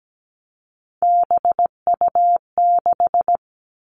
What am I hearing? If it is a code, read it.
Morse code, heard as a keyed tone, about 17 wpm, 700 Hz. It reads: BU6
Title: Three Challenging Letter-Number Combinations 17wpm